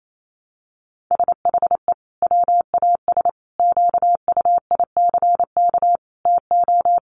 Morse code, heard as a keyed tone, about 28 wpm, 700 Hz. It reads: SHE WAS QUICK TO
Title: Sentences from Top 1200 Words 28wpm